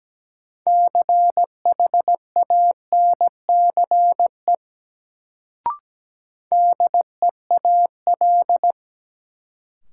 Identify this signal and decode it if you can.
Morse code, heard as a keyed tone, about 17 wpm, 700 Hz. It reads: CHANCE DEAL
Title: Sets of 3 Words - New Word Drill - 17wpm